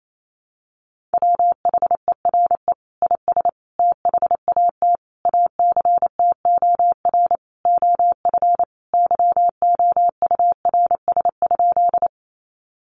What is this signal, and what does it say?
Morse code, heard as a keyed tone, about 28 wpm, 700 Hz. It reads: WHERE IS THAT ACTOR OF YOURS?
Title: Sentences from Top 1500 Words 28wpm